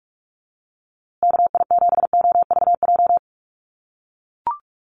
Morse code, heard as a keyed tone, about 44 wpm, 700 Hz. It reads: KI7OVJ
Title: USA Call Signs Repeated 3x 45wpm